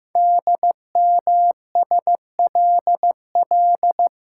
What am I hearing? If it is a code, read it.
Morse code, heard as a keyed tone, about 15 wpm, 700 Hz. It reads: DMSLL
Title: Code Groups - 5 Letter:Numbers - 15wpm